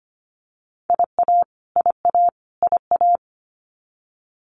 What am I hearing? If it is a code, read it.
Morse code, heard as a keyed tone, about 25 wpm, 700 Hz. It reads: IA IA IA
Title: US States Repeated 3x 25wpm